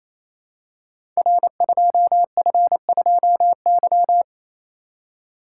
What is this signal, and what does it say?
Morse code, heard as a keyed tone, about 29 wpm, 700 Hz. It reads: R2F2Y
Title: Code Groups - 5 Letter:Numbers - 28wpm